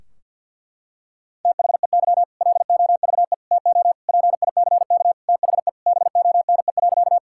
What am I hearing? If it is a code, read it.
Morse code, heard as a keyed tone, about 50 wpm, 700 Hz. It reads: THEY LOVE TO PICK THE BONE.
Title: Sentences from Top 1600 Words 50wpm